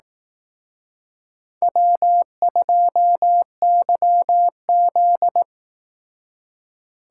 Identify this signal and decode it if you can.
Morse code, heard as a keyed tone, about 18 wpm, 700 Hz. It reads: W2YZ
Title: Speed Racing - USA Call Signs 15wpm